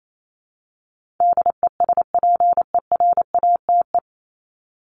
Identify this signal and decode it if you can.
Morse code, heard as a keyed tone, about 28 wpm, 700 Hz. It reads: DESPERATE
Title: Top 1751-2000 Words 28wpm